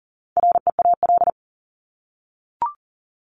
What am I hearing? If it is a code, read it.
Morse code, heard as a keyed tone, about 40 wpm, 700 Hz. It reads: REAL
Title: Top 300 Words 40wpm